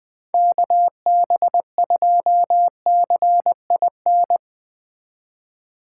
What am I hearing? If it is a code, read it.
Morse code, heard as a keyed tone, about 20 wpm, 700 Hz. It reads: KB2CIN
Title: Speed Racing - USA 2x3 Call Signs 17wpm